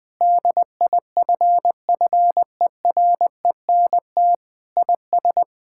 Morse code, heard as a keyed tone, about 20 wpm, 700 Hz. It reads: DIFFERENT IS
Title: Sentences with 8-Word Phrases - 20wpm